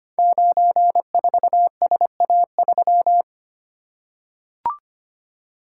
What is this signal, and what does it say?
Morse code, heard as a keyed tone, about 25 wpm, 700 Hz. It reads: 94SA3
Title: Code Groups - 5 Letter:Numbers - 25wpm